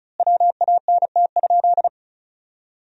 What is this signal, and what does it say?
Morse code, heard as a keyed tone, about 36 wpm, 700 Hz. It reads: WANT?
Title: 3x Word-Spacing - Sentences from Top 200 Words - 35wpm